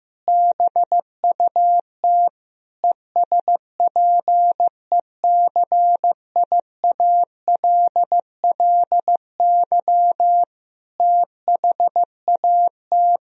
Morse code, heard as a keyed tone, about 15 wpm, 700 Hz. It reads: BUT ESPECIALLY THAT
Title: Sentences from Top 1000 Words - Review Part 4 - 15wpm